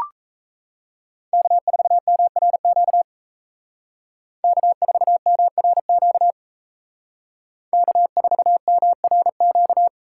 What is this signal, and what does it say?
Morse code, heard as a keyed tone, about 36 wpm, 700 Hz. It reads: K4MRQ K4MRQ K4MRQ